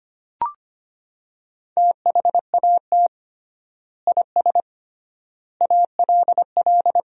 Morse code, heard as a keyed tone, about 25 wpm, 700 Hz. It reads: THAT IS ALL
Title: 3x Word-Spacing - Sentences from Top 200 Words - 25wpm